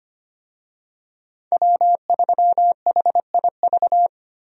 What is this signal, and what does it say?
Morse code, heard as a keyed tone, about 25 wpm, 700 Hz. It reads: W3HIV
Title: USA Call Signs - 5 Characters - 25wpm